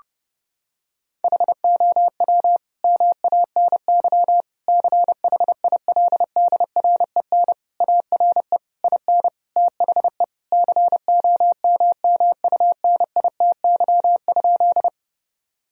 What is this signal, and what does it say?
Morse code, heard as a keyed tone, about 32 wpm, 700 Hz. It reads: HOW MANY CHILDREN ARE IN THE COMMUNITY?
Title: Sentences from Top 1000 Words - Review Part 4 - 30wpm